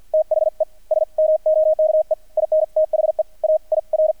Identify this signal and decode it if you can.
Morse code, heard as a keyed tone, about 48 wpm, spaced as for 42 wpm, 620 Hz, with radio noise. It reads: THE SMOKE IN THE AIR
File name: archived / CRSnet